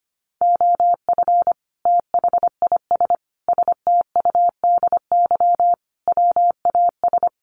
Morse code, heard as a keyed tone, about 25 wpm, 700 Hz. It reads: OF THIS STUDY WAS
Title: Sets of 5 Words 25wpm